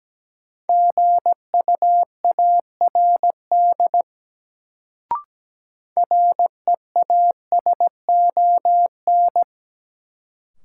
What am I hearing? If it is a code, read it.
Morse code, heard as a keyed tone, about 17 wpm, 700 Hz. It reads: GUARD REASON
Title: Sentences from Top 700 Words - New Word Drill - 17wpm